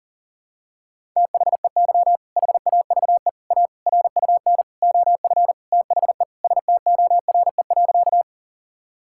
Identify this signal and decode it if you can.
Morse code, heard as a keyed tone, about 37 wpm, 700 Hz. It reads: THEY HAVE A RUN OF THE STORE.
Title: Sentences from Top 1000 Words - Review Part 5 - 40wpm